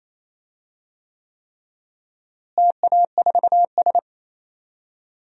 Morse code, heard as a keyed tone, about 28 wpm, 700 Hz. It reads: TA4S
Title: DXCC-100 Call Signs - 4 Characters - Repeated 3x - 28wpm